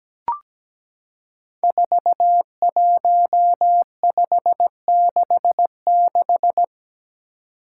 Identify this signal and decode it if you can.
Morse code, heard as a keyed tone, about 17 wpm, 700 Hz. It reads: 41566
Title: Five Challenging Letter-Number Combinations 17wpm